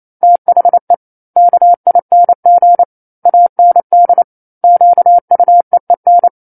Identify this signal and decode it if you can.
Morse code, heard as a keyed tone, about 28 wpm, 700 Hz. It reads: THE KING AND QUEEN
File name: 09-Chapter-09 28 WPM